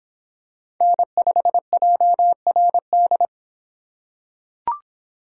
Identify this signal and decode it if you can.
Morse code, heard as a keyed tone, about 25 wpm, 700 Hz. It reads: N5JRD